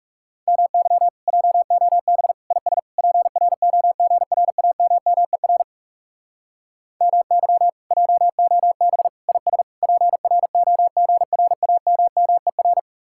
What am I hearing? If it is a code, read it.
Morse code, heard as a keyed tone, about 40 wpm, 700 Hz. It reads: MY JOB IS PROGRAMMER MY JOB IS PROGRAMMER